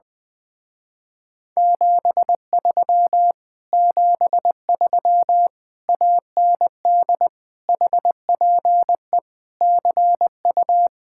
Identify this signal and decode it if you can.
Morse code, heard as a keyed tone, about 20 wpm, 700 Hz. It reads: 73 73 AND HPE CU
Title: Speed-Racing Common QSO Phrases - 20wpm